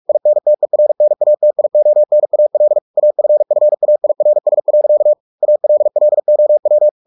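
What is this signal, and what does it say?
Morse code, heard as a keyed tone, about 44 wpm, 600 Hz. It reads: INTERNATIONAL AFFAIRS. ALLOW
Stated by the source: Morse practice at 45 wpm, the novel Never by Ken Follett in English, from CW med Gnister.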